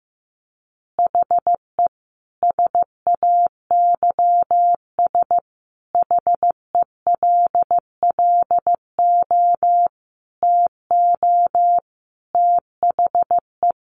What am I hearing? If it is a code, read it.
Morse code, heard as a keyed tone, about 15 wpm, 700 Hz. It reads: HE SAYS HELLO TO THE
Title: Easy Phrases 15wpm